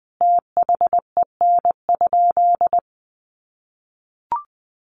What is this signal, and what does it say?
Morse code, heard as a keyed tone, about 20 wpm, 700 Hz. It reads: THEN?
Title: Sentences from Top 1300 Words 20wpm